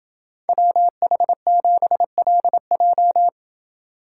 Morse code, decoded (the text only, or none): WH7LJ